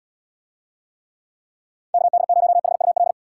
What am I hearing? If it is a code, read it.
Morse code, heard as a keyed tone, about 78 wpm, 700 Hz. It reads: KD0HUL